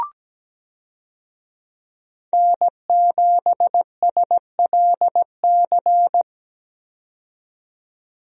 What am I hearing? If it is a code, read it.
Morse code, heard as a keyed tone, about 17 wpm, 700 Hz. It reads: N7SLC
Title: USA Call Signs - 5 Characters - 17wpm